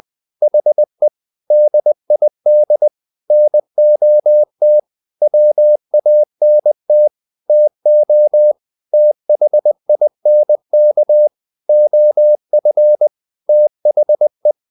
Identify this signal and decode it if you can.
Morse code, heard as a keyed tone, about 20 wpm, 600 Hz. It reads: HE DID NOT WANT TO THINK OF THE
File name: the old man and the sea part 2 20wpm